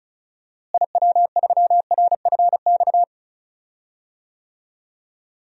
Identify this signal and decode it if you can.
Morse code, heard as a keyed tone, about 36 wpm, 700 Hz. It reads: IW3RFX
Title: DXCC-100 Call Signs - 6 Characters - Repeated 2x - 35wpm